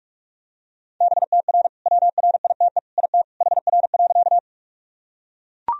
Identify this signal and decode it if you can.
Morse code, heard as a keyed tone, about 41 wpm, 700 Hz. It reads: BTR WRITE IT HR.
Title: Sentences from Top 400 Words and Abbreviations 45wpm